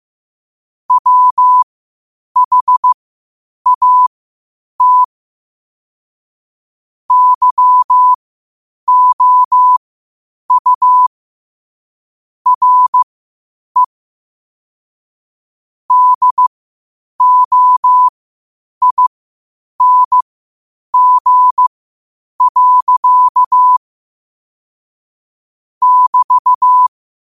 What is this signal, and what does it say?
Morse code, heard as a keyed tone, about 15 wpm, spaced as for 8 wpm, 1000 Hz. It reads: WHAT YOU RE DOING. <BT>
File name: qotdfiles / 8wpm